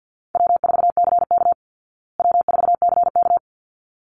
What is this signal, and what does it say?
Morse code, heard as a keyed tone, about 50 wpm, 700 Hz. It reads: W4CK W4CK
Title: Easy Call Signs Repeated 2x - 50wpm